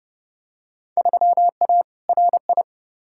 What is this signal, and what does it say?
Morse code, heard as a keyed tone, about 30 wpm, 700 Hz. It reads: 3A RI
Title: Field Day Exchanges - 30wpm